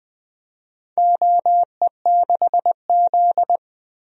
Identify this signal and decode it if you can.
Morse code, heard as a keyed tone, about 20 wpm, 700 Hz. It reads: OE6Z
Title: DXCC-100 Call Signs - 4 Characters - Repeated 2x - 20wpm